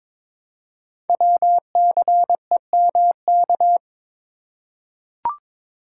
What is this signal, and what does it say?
Morse code, heard as a keyed tone, about 23 wpm, 700 Hz. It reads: WCEMK
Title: Code Groups - 5 Letters - 22wpm